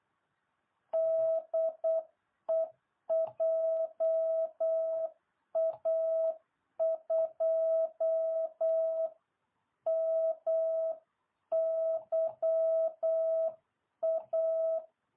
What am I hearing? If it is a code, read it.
Morse code, heard as a keyed tone, about 8 wpm, 660 Hz. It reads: DE JA2MYA